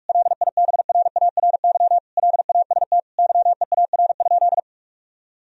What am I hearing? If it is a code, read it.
Morse code, heard as a keyed tone, about 42 wpm, 700 Hz. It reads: LIBRARY LAST YEAR?
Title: Sentences from Top 1000 Words - Review Part 4 - 45wpm